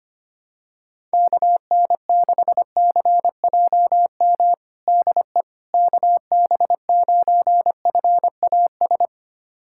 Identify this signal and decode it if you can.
Morse code, heard as a keyed tone, about 25 wpm, 700 Hz. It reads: KN6CJM DE KB9FAS
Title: USA Callsign DE Callsign 25wpm